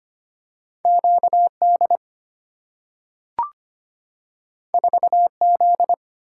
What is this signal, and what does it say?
Morse code, heard as a keyed tone, about 25 wpm, 700 Hz. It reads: Q D 4 Z
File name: Two Letter-Number Combinations - 25wpm